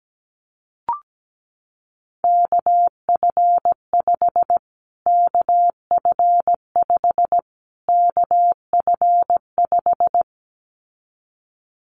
Morse code, heard as a keyed tone, about 17 wpm, 700 Hz. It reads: KF5 KF5 KF5
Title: Callsign Prefixes for Top 25 DXCC - Repeated 3x - 17wpm